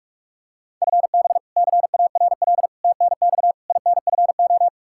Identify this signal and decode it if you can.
Morse code, heard as a keyed tone, about 46 wpm, 700 Hz. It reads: FB CARL TNX INFO
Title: Practice QSO - Exchange 3 - More Info - 45wpm